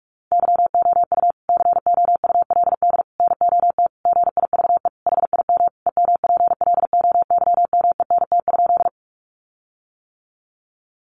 Morse code, heard as a keyed tone, about 44 wpm, 700 Hz. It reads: YOU COULD NOT GIVE HIM EMPLOYMENT?
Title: Sentences from Top 1500 Words 45wpm